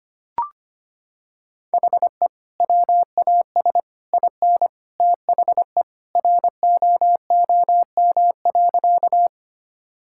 Morse code, heard as a keyed tone, about 25 wpm, 700 Hz. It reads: HE WAS IN THE ROOM.
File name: practice / 25wpm / Sentences with 2-Word Phrases - 25wpm